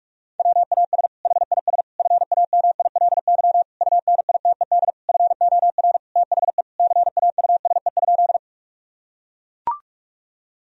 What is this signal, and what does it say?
Morse code, heard as a keyed tone, about 43 wpm, 700 Hz. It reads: WAS HIS FAMILY UNITED FOR THE CAUSE?
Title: Sentences from Top 1000 Words - Review Part 4 - 45wpm